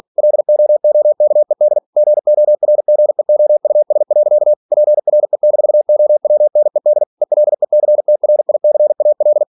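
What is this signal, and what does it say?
Morse code, heard as a keyed tone, about 46 wpm, 600 Hz. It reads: LOOKED GORGEOUS. PRE-OWNED ELECTRICAL
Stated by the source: Practice text from an English novel sent at 47 WPM.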